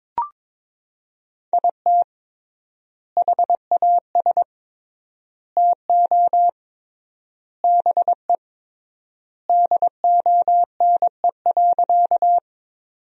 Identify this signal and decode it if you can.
Morse code, heard as a keyed tone, about 21 wpm, 700 Hz. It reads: IT HAS TO BE DONE.
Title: 3x Word-Spacing - Sentences from Top 500 Words - 22wpm